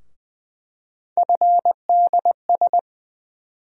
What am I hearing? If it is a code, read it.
Morse code, heard as a keyed tone, about 20 wpm, 700 Hz. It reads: FDS
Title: USA Call Sign Postfixes - 3 Letters 20wpm